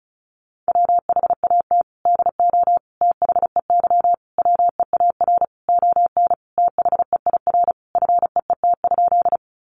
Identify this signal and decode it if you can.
Morse code, heard as a keyed tone, about 34 wpm, 700 Hz. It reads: WHAT DO THEY WEAR ON THEIR FEET?
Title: Sentences from Top 1500 Words 35wpm